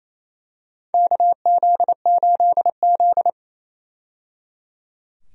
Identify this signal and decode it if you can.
Morse code, heard as a keyed tone, about 28 wpm, 700 Hz. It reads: KZ8Z
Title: USA Call Signs - 4 Characters - 28wpm